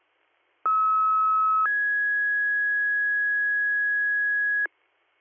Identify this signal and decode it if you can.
Two-tone page, heard as a one-second tone, then a three-second tone, about 1290 then 1650 Hz, with noise. It is heard from 0.7 to 4.7 s.